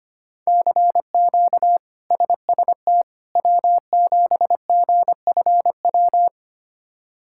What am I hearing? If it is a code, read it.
Morse code, heard as a keyed tone, about 25 wpm, 700 Hz. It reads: CQ SST W7GFW
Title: K1USN SST QSO Practice - 25wpm